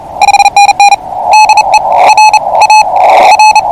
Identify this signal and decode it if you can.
Morse code, heard as a keyed tone, about 30 wpm, 2550 Hz, with radio noise. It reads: DE R A R
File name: C7A 14032 kHz CW